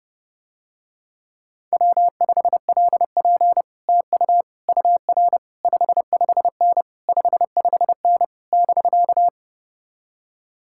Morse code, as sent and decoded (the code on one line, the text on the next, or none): .-- ..... .-.. .--. / - ..- / ..- .-. / ..... ..... -. / ..... ..... -. / -...-.-
W5LP TU UR 55N 55N <BK>